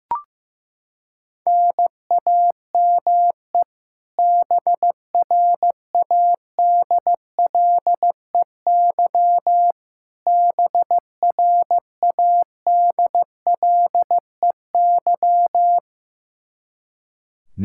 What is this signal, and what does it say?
Morse code, heard as a keyed tone, about 16 wpm, 700 Hz. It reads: NAME BRADLEY BRADLEY
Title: Practice QSO - Exchange 1 - Beginning - 15wpm